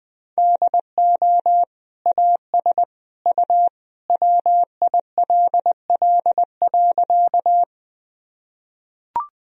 Morse code, heard as a keyed tone, about 20 wpm, 700 Hz. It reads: DO AS U WILL.
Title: Sentences from Top 100 Words and Abbreviations 20wpm